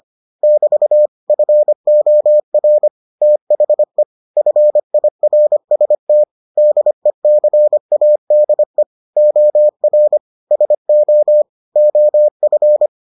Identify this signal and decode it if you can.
Morse code, heard as a keyed tone, about 25 wpm, 600 Hz. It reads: <BT> FOR THE FIRST DECADE OR SO OF